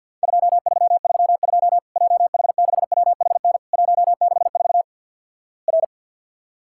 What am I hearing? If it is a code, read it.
Morse code, heard as a keyed tone, about 50 wpm, 700 Hz. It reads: 2332 JH6WHN 164 R
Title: Worked All Europe DX Contest - QTC Exchange with Repeats - 50wpm